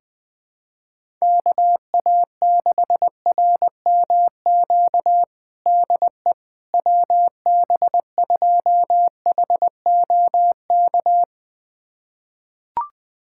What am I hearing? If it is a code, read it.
Morse code, heard as a keyed tone, about 20 wpm, 700 Hz. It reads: KA6RMQ DE WB2HOK
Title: USA Callsign DE Callsign - 6 Characters - 20wpm